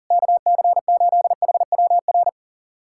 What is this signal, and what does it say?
Morse code, heard as a keyed tone, about 40 wpm, 700 Hz. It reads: KC8HWR
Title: USA Callsign DE Callsign 40wpm